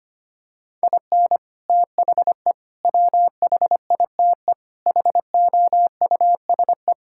Morse code, heard as a keyed tone, about 25 wpm, 700 Hz. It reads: IN THE WHITE HOUSE
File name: practice / 25wpm / Sentences with 4-Word Phrases - 25wpm